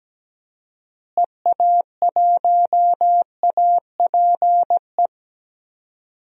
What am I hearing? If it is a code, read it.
Morse code, heard as a keyed tone, about 17 wpm, 700 Hz. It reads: EA1APE
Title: DXCC-100 Call Signs - 6 Characters - Repeated 2x - 17wpm